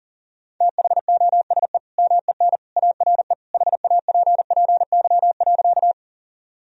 Morse code, heard as a keyed tone, about 40 wpm, 700 Hz. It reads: THOSE MEN ARE HAPPY.
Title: Sentences from Top 1000 Words - Review Part 1 - 40wpm